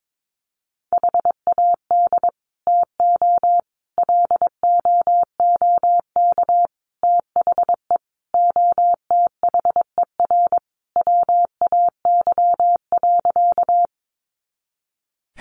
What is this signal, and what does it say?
Morse code, heard as a keyed tone, about 22 wpm, 700 Hz. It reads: HAD TO LOOK THE OTHER WAY.
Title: Sentences from Top 100 Words 22wpm